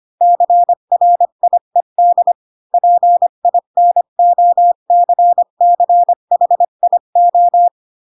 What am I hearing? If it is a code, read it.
Morse code, heard as a keyed tone, about 25 wpm, 700 Hz. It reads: CRIED PINOCCHIO
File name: Pinocchio / Pinocchio ch7 20 25 ik1hge